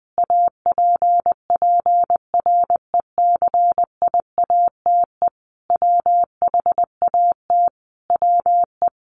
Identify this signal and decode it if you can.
Morse code, heard as a keyed tone, about 20 wpm, 700 Hz. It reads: APPRECIATE WHAT WE
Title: Sentences from Top 1300 Words 20wpm